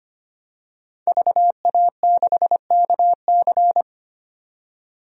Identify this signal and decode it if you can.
Morse code, heard as a keyed tone, about 25 wpm, 700 Hz. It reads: VA6KC
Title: DXCC-100 Call Signs - 5 Characters - 25wpm